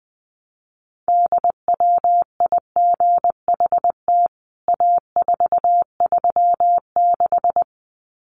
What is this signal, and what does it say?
Morse code, heard as a keyed tone, about 19 wpm, 700 Hz. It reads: DWIGHT A436
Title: CWT Exchange 20wpm